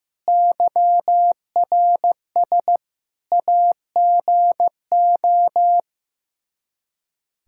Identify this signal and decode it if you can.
Morse code, heard as a keyed tone, about 15 wpm, 700 Hz. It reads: YRS AGO
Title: Practice QSO - Exchange 3 - More Info - 15wpm